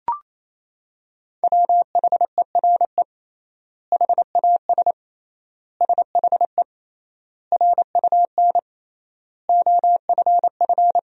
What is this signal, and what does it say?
Morse code, heard as a keyed tone, about 28 wpm, 700 Hz. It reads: WHERE HAS SHE RUN OFF
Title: 3x Word-Spacing - Sentences from Top 300 Words - 28wpm